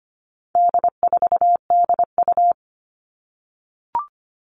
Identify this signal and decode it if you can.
Morse code, heard as a keyed tone, about 25 wpm, 700 Hz. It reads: D4DU
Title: All Cut Numbers - 4-Digit 25wpm